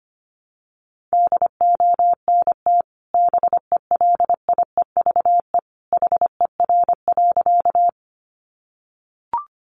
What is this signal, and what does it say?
Morse code, heard as a keyed tone, about 25 wpm, 700 Hz. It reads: DONT BELIEVE HER.
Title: Sentences from Top 1000 Words 25wpm